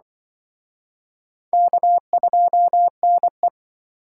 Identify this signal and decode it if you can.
Morse code, heard as a keyed tone, about 26 wpm, 700 Hz. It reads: K2NE